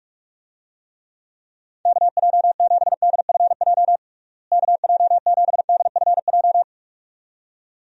Morse code, heard as a keyed tone, about 46 wpm, 700 Hz. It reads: KJ7DFJ KJ7DFJ
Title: USA Call Signs - 6 Characters - Repeated 2x - 45wpm